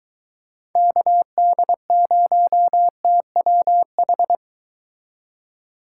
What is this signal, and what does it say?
Morse code, heard as a keyed tone, about 23 wpm, 700 Hz. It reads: KD0TWH